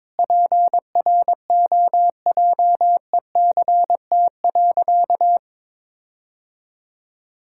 Morse code, heard as a keyed tone, about 22 wpm, 700 Hz. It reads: PROJECT.
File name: practice / 22wpm / Sentences with 4-Word Phrases - 22wpm